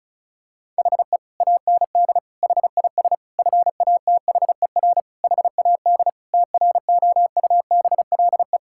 Morse code, heard as a keyed tone, about 34 wpm, 700 Hz. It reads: HE AND HIS FATHER HAD TROUBLE
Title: Sentences from Top 1600 Words 35wpm